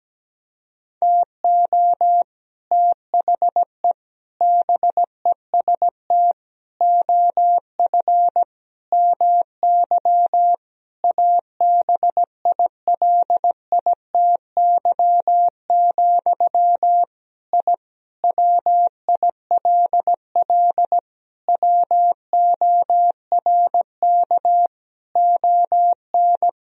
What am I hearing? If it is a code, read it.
Morse code, heard as a keyed tone, about 17 wpm, 700 Hz. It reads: TO THE BEST OF MY ABILITY, I WILL WORK ON
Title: Sentences from Top 1000 Words - Review Part 3 - 17wpm